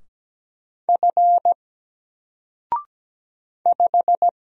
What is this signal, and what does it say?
Morse code, heard as a keyed tone, about 17 wpm, 700 Hz. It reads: F 5
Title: Single Letter-Number Alt 17wpm